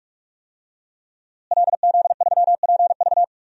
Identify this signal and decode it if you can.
Morse code, heard as a keyed tone, about 47 wpm, 700 Hz. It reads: LZ3PV
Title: DXCC-100 Call Signs - 5 Characters - Repeated 2x - 45wpm